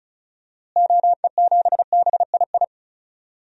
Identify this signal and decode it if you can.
Morse code, heard as a keyed tone, about 36 wpm, 700 Hz. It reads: OE7BII
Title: DXCC-100 Call Signs - 6 Characters - Repeated 3x - 35wpm